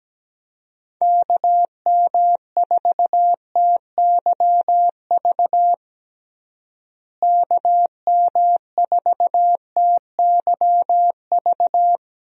Morse code, heard as a keyed tone, about 17 wpm, 700 Hz. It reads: KM4TYV KM4TYV
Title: USA Call Signs - 6 Characters - Repeated 3x - 17wpm